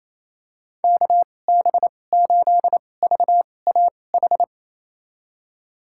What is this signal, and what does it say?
Morse code, heard as a keyed tone, about 28 wpm, spaced as for 20 wpm, 700 Hz. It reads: KB8VAH
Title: USA Call Signs - 6 Characters - 2x Character Spacing - 28wpm